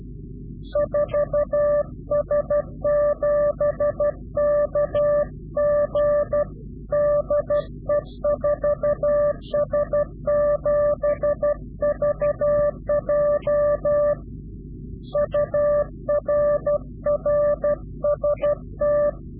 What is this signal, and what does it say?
Morse code, heard as a keyed tone, about 12 wpm, 590 Hz. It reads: 4S7KG DE 4S7VJ UR RST
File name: CW-QSO / Directional Call / Directional QSO